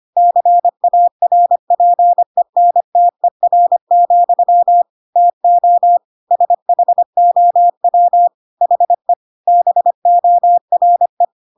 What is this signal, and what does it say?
Morse code, heard as a keyed tone, about 25 wpm, 700 Hz. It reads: CARPENTER, TO SHOW HE BORE
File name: Pinocchio / Pinocchio ch2 25 25 ik1hge